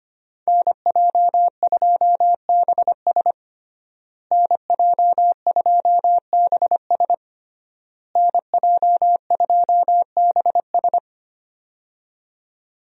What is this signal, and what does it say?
Morse code, heard as a keyed tone, about 25 wpm, 700 Hz. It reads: NJ2BS NJ2BS NJ2BS
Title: USA Call Signs - 5 Characters - Repeated 3x - 25wpm